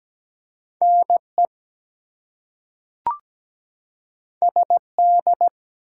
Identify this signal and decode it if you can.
Morse code, heard as a keyed tone, about 17 wpm, 700 Hz. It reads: NE SD
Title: US States and Canadian Provinces 17wpm